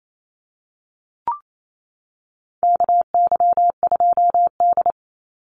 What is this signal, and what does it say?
Morse code, heard as a keyed tone, about 28 wpm, 700 Hz. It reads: KY2D
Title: Call Signs - 28wpm